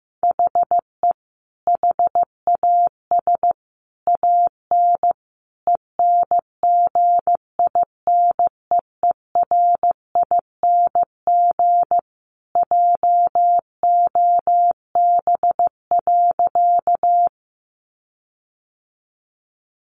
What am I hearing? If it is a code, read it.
Morse code, heard as a keyed tone, about 15 wpm, 700 Hz. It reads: HE HAS AN ENGINEERING JOB.
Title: Sentences from Top 1500 Words 15wpm